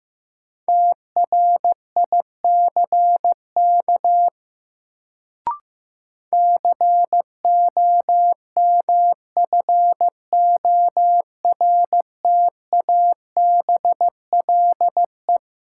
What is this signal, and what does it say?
Morse code, heard as a keyed tone, about 15 wpm, 700 Hz. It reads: TRICK COMFORTABLE
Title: Sentences from Top 1500 Words - New Word Drill Alt - 15wpm